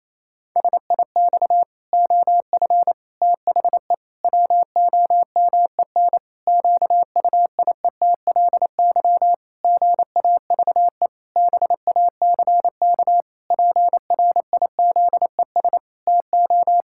Morse code, heard as a keyed tone, about 28 wpm, 700 Hz. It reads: SIX OF THE WOMEN QUIETLY GAVE BACK PRIZES TO
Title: Pangram Challenge v2 - 28wpm